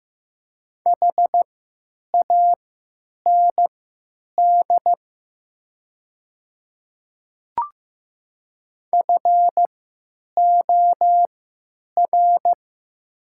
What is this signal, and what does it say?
Morse code, heard as a keyed tone, about 15 wpm, spaced as for 9 wpm, 700 Hz. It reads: HAND FOR
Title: 3x Character Spacing - Top 200 Words - 15wpm